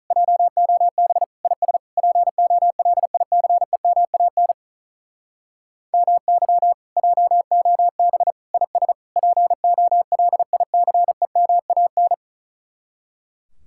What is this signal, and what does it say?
Morse code, heard as a keyed tone, about 45 wpm, 700 Hz. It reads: JOB IS POLICEMAN MY JOB IS POLICEMAN